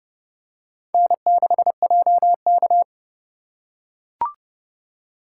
Morse code, heard as a keyed tone, about 29 wpm, 700 Hz. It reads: N6JK